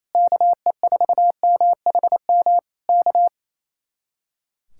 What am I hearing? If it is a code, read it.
Morse code, heard as a keyed tone, about 27 wpm, 700 Hz. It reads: KE4MHM K
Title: Speed-Racing Common QSO Phrases Encore - 28wpm